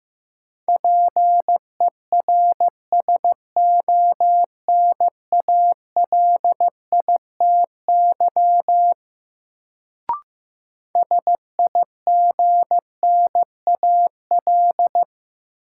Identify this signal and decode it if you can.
Morse code, heard as a keyed tone, about 15 wpm, 700 Hz. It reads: PERSONALITY SIGNAL
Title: Sentences from Top 1400 Words - New Word Drill - 15wpm